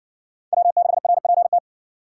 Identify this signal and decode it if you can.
Morse code, heard as a keyed tone, about 57 wpm, 700 Hz. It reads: W6RPT